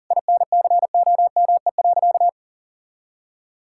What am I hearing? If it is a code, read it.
Morse code, heard as a keyed tone, about 40 wpm, 700 Hz. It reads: INCOME.
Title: Sentences from Top 1000 Words - Review Part 5 - 40wpm